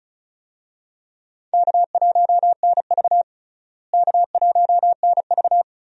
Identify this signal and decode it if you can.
Morse code, heard as a keyed tone, about 34 wpm, 700 Hz. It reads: K1NV K1NV
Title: USA Call Signs - 4 Characters - Repeated 3x - 35wpm